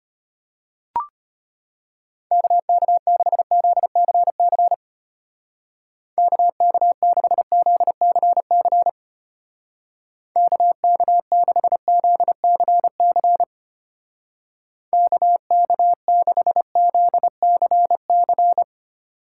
Morse code, heard as a keyed tone, about 32 wpm, 700 Hz. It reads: KK6ZCC KK6ZCC KK6ZCC KK6ZCC